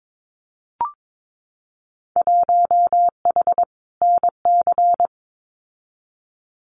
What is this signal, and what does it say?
Morse code, heard as a keyed tone, about 22 wpm, 700 Hz. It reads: 1H NC
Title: Winter Field Day Exchanges 22wpm